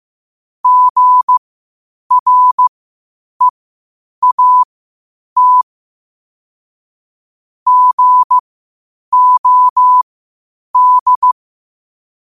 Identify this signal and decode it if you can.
Morse code, heard as a keyed tone, about 15 wpm, spaced as for 8 wpm, 1000 Hz. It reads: GREAT GOD
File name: qotdfiles / 8wpm